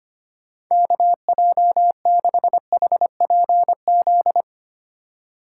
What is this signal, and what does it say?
Morse code, heard as a keyed tone, about 25 wpm, 700 Hz. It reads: KJ6HPZ
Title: USA Call Signs - 6 Characters - 2x Character Spacing - 25wpm